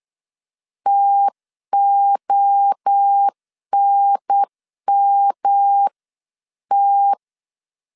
Morse code, heard as a keyed tone, about 8 wpm, 800 Hz. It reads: TONMT